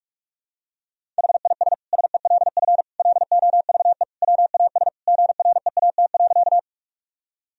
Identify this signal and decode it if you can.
Morse code, heard as a keyed tone, about 46 wpm, 700 Hz. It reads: HIS SELF LOVE WAS GREAT.
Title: Sentences from Top 1000 Words - Review Part 1 - 45wpm